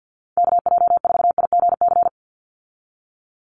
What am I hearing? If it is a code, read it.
Morse code, heard as a keyed tone, about 52 wpm, 700 Hz. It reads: KJ4IGC
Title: USA Call Signs 50wpm